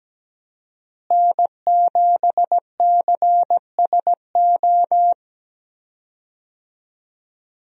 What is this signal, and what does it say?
Morse code, heard as a keyed tone, about 17 wpm, 700 Hz. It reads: N7CSO
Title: USA Call Signs - 5 Characters - Repeated 3x - 17wpm